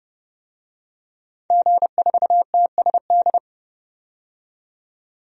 Morse code, heard as a keyed tone, about 30 wpm, 700 Hz. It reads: G4TSD
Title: DXCC-100 Call Signs - 5 Characters - 30wpm